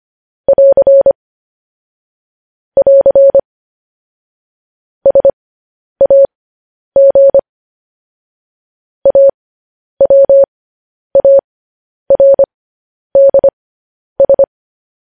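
Morse code, heard as a keyed tone, about 25 wpm, spaced as for 10 wpm, 550 Hz. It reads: <AR> <AR> SAG AWARDS
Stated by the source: Morse code news headlines practice transmission.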